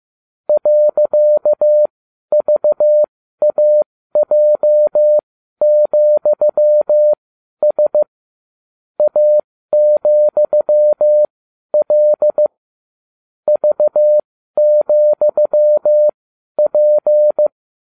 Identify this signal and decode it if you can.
Morse code, heard as a keyed tone, about 15 wpm, spaced as for 12 wpm, 600 Hz. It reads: .VAJ,S A,L V,P